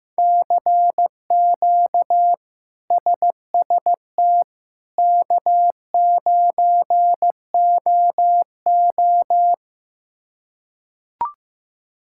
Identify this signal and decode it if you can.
Morse code, heard as a keyed tone, about 15 wpm, 700 Hz. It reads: CQ SST K9OO
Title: K1USN SST QSO Practice - 15wpm